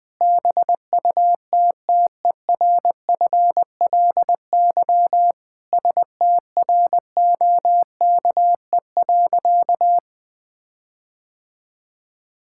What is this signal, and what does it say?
Morse code, heard as a keyed tone, about 20 wpm, 700 Hz. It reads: BUTTERFLY STROKE.
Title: All-Age Appropriate Jokes - 20wpm